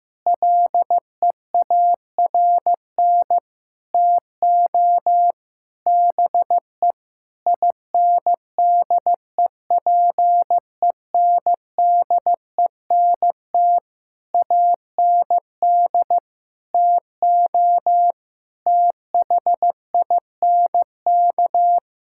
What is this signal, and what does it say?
Morse code, heard as a keyed tone, about 15 wpm, 700 Hz. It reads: LEARN TO BE INDEPENDENT AND TO THINK
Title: Sentences from Top 1000 Words - Review Part 6 - 15wpm